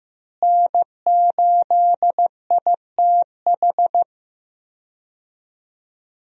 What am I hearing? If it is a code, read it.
Morse code, heard as a keyed tone, about 15 wpm, 700 Hz. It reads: N8ITH
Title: USA Call Signs - 5 Characters - Repeated 3x - 15wpm